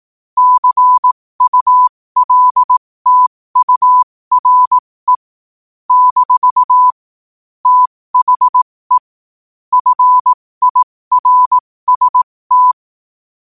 Morse code, heard as a keyed tone, about 18 wpm, spaced as for 15 wpm, 1000 Hz. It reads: CULTURE - THE FIRST
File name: qotdfiles / 15wpm